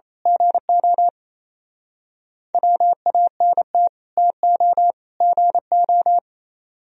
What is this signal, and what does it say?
Morse code, heard as a keyed tone, about 29 wpm, 700 Hz. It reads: GO WANT TO GO
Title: Speed-Racing - Sets of 3 Words 28wpm